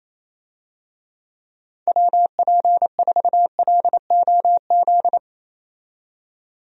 Morse code, heard as a keyed tone, about 28 wpm, 700 Hz. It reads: WP4LOZ
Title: USA Call Signs - 6 Characters - 28wpm